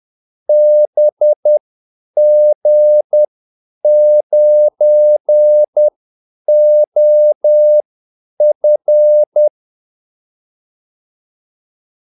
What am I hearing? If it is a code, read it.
Morse code, heard as a keyed tone, about 10 wpm, spaced as for 8 wpm, 600 Hz. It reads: BG9OF
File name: Callsigns-10-8wpm